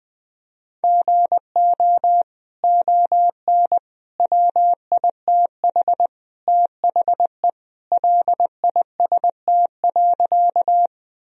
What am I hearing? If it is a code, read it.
Morse code, heard as a keyed tone, about 20 wpm, 700 Hz. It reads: GO ON WITH THE LIST.
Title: Sentences from Top 1000 Words - Review Part 2 - 20wpm